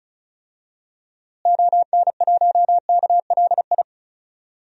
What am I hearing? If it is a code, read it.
Morse code, heard as a keyed tone, about 36 wpm, 700 Hz. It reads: ON1KLI
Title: DXCC-100 Call Signs - 6 Characters - Repeated 3x - 35wpm